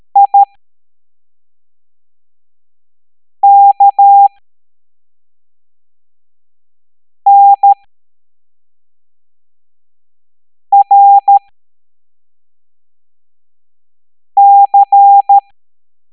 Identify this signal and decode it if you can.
Morse code, heard as a keyed tone, about 13 wpm, spaced as for 3 wpm, 800 Hz. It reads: IKNRC
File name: hamdata / L03-EX01-IGC